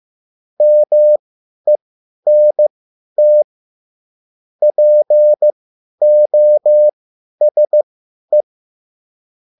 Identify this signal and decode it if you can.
Morse code, heard as a keyed tone, about 15 wpm, spaced as for 10 wpm, 600 Hz. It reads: MENT POSE